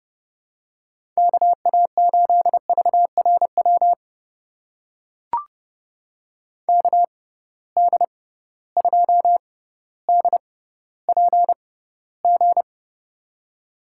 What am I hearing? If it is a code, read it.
Morse code, heard as a keyed tone, about 31 wpm, spaced as for 16 wpm, 700 Hz. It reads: KA8VRW KD2DPG
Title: USA Call Signs - 6 Characters - 6x Character Spacing - 30wpm